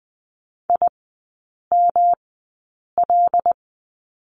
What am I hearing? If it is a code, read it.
Morse code, heard as a keyed tone, about 20 wpm, spaced as for 8 wpm, 700 Hz. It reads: IML
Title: 107 Lesson 18e - Letter m - Three Characters 20wpm